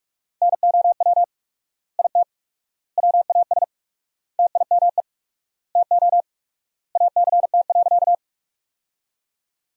Morse code, heard as a keyed tone, about 44 wpm, 700 Hz. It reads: NOW IT WAS TIME TO ACT.
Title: 4x Word-Spacing - Sentences from Top 200 Words - 45wpm